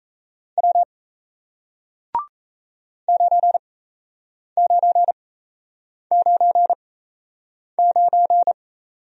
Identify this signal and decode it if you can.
Morse code, heard as a keyed tone, about 38 wpm, spaced as for 8 wpm, 700 Hz. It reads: W 9999